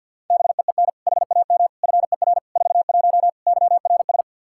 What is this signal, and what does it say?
Morse code, heard as a keyed tone, about 52 wpm, 700 Hz. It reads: BEEN HAM FER 41 YRS
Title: Practice QSO - Exchange 3 - More Info - 50wpm